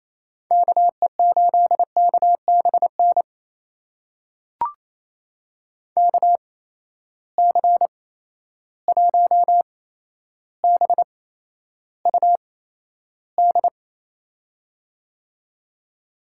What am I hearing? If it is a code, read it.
Morse code, heard as a keyed tone, about 28 wpm, spaced as for 12 wpm, 700 Hz. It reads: KE8KBN KC1BUD
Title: USA Call Signs - 6 Characters - 8x Character Spacing - 28wpm